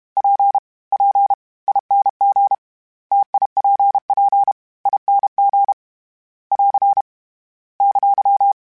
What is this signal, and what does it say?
Morse code, heard as a keyed tone, about 32 wpm, spaced as for 26 wpm, 800 Hz. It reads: PPING TIPPING <AR> !
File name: Difficult Combos / Difficult Combos-mp3s / T AOEI @30